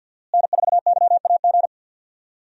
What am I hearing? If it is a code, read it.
Morse code, heard as a keyed tone, about 52 wpm, 700 Hz. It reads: N4YAG
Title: USA Call Signs - 5 Characters - Repeated 3x - 50wpm